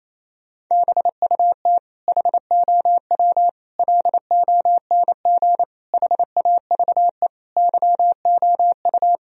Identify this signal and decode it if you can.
Morse code, heard as a keyed tone, about 28 wpm, 700 Hz. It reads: BUT HOW LONG HAVE YOU
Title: Sentences from Top 1000 Words - Review Part 3 - 28wpm